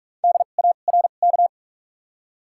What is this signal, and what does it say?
Morse code, heard as a keyed tone, about 46 wpm, spaced as for 32 wpm, 700 Hz. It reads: DARK
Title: 2x Character Spacing - Top 500 Words - 45wpm